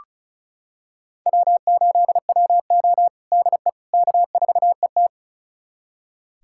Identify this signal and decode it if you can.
Morse code, heard as a keyed tone, about 34 wpm, 700 Hz. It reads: W8WO DE K4ET
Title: USA Callsign DE Callsign - 4 Characters - 35wpm